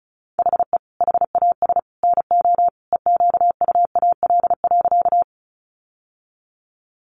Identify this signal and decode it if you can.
Morse code, heard as a keyed tone, about 34 wpm, 700 Hz. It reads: HE HAS NO EQUAL.
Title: Sentences from Top 1400 Words 35wpm